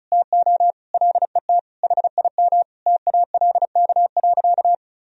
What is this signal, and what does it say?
Morse code, heard as a keyed tone, about 34 wpm, 700 Hz. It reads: TO LET HIM TALK.
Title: Sentences from Top 1000 Words - Review Part 4 - 35wpm